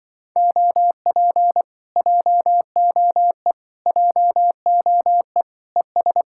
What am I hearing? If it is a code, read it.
Morse code, heard as a keyed tone, about 24 wpm, 700 Hz. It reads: OP JOE JOE ES
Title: Speed-Racing Common QSO Phrases Encore - 20wpm